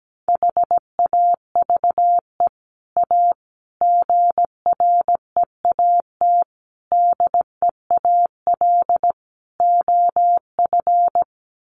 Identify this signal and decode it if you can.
Morse code, heard as a keyed tone, about 17 wpm, 700 Hz. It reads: HAVE A GREAT DEAL OF
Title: Sets of 5 Words Encore2 17wpm